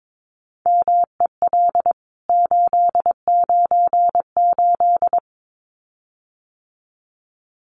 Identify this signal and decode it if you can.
Morse code, heard as a keyed tone, about 22 wpm, 700 Hz. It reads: MEL 898
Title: CWT Exchange 22wpm